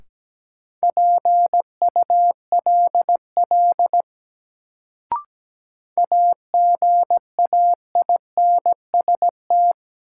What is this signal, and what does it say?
Morse code, heard as a keyed tone, about 17 wpm, 700 Hz. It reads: PULL AGAINST
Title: Sentences from Top 400 Words - New Word Drill - 17wpm